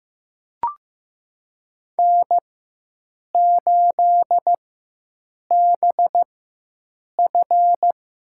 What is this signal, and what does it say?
Morse code, heard as a keyed tone, about 15 wpm, spaced as for 7 wpm, 700 Hz. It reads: N8BF
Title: USA Call Signs - 4x Character Spacing - 15wpm